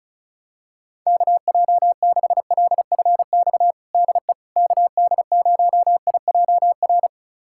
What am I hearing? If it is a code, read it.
Morse code, heard as a keyed tone, about 36 wpm, 700 Hz. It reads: KJ6LFX DE KD0IJR
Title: USA Callsign DE Callsign - 6 Characters - 35wpm